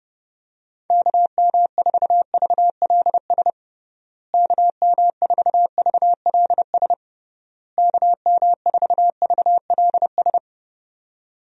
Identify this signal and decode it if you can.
Morse code, heard as a keyed tone, about 30 wpm, 700 Hz. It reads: KM4VLS KM4VLS KM4VLS
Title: USA Call Signs - 6 Characters - Repeated 3x - 30wpm